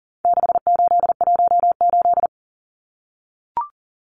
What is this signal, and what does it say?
Morse code, heard as a keyed tone, about 40 wpm, 700 Hz. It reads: 6818
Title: Numbers - 4-Digit 40wpm